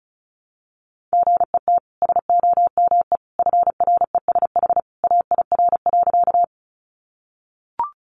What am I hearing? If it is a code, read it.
Morse code, heard as a keyed tone, about 34 wpm, 700 Hz. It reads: GET SOME FRESH AIR.
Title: Sentences from Top 1200 Words 35wpm